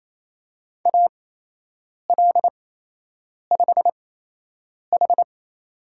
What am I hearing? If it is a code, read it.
Morse code, heard as a keyed tone, about 28 wpm, spaced as for 8 wpm, 700 Hz. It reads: AL5H
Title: USA Call Signs - 4 Characters - 8x Character Spacing - 28wpm